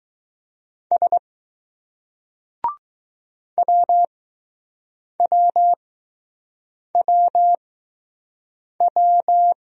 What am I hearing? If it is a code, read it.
Morse code, heard as a keyed tone, about 20 wpm, spaced as for 6 wpm, 700 Hz. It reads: S WWWW